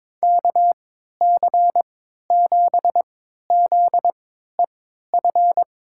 Morse code, heard as a keyed tone, about 22 wpm, spaced as for 12 wpm, 700 Hz. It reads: KC7ZEF
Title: USA Call Signs - 6 Characters - 3x Character Spacing - 22wpm